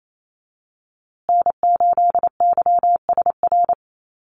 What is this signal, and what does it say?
Morse code, heard as a keyed tone, about 28 wpm, 700 Hz. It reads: N8YSR
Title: USA Call Signs Repeated 3x 28wpm